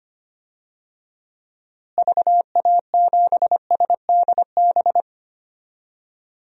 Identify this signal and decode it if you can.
Morse code, heard as a keyed tone, about 25 wpm, 700 Hz. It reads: VA7SDB
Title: DXCC-100 Call Signs - 6 Characters - Repeated 2x - 25wpm